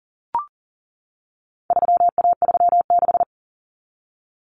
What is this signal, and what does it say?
Morse code, heard as a keyed tone, about 40 wpm, 700 Hz. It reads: 3A36
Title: Common Cut Numbers - 4-Digit 40wpm